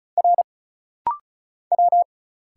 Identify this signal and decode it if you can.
Morse code, heard as a keyed tone, about 34 wpm, 700 Hz. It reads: R W
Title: Single Letters - Rapid-Fire 35wpm